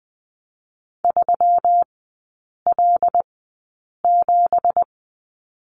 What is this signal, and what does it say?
Morse code, heard as a keyed tone, about 20 wpm, spaced as for 8 wpm, 700 Hz. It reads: 3L7
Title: 166 Lesson 27e - Number 7 - Three Characters 20wpm